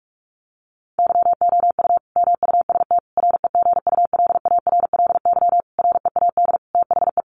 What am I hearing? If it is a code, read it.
Morse code, heard as a keyed tone, about 44 wpm, 700 Hz. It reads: YOU MUST REGULARLY READ THE
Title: Sentences from Top 1400 Words 45wpm